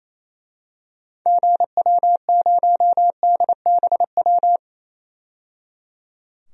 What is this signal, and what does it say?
Morse code, heard as a keyed tone, about 28 wpm, 700 Hz. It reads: GW0DBW
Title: DXCC-100 Call Signs - 6 Characters - 28wpm